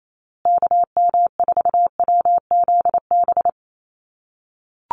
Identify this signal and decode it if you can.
Morse code, heard as a keyed tone, about 28 wpm, 700 Hz. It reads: KM4WZB
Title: USA Call Signs Repeated 3x 28wpm